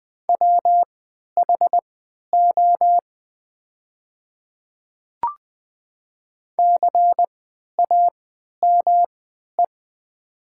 Morse code, heard as a keyed tone, about 20 wpm, spaced as for 11 wpm, 700 Hz. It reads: WHO CAME
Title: 3x Character Spacing - Top 200 Words - 20wpm